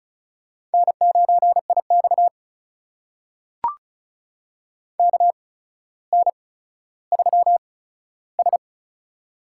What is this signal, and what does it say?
Morse code, heard as a keyed tone, about 36 wpm, spaced as for 15 wpm, 700 Hz. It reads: N9IX KN3S
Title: USA Call Signs - 4 Characters - 8x Character Spacing - 35wpm